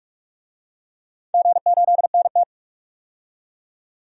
Morse code, heard as a keyed tone, about 46 wpm, 700 Hz. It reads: G8NT
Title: DXCC-100 Call Signs - 4 Characters - 45wpm